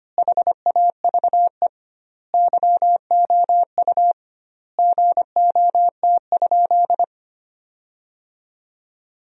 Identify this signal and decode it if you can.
Morse code, heard as a keyed tone, about 25 wpm, 700 Hz. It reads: HAVE YOU GOT?
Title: 2x Word-Spacing - Sentences from Top 300 Words - 25wpm